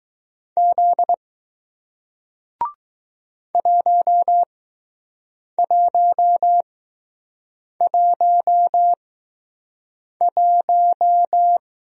Morse code, heard as a keyed tone, about 20 wpm, spaced as for 6 wpm, 700 Hz. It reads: Z 1111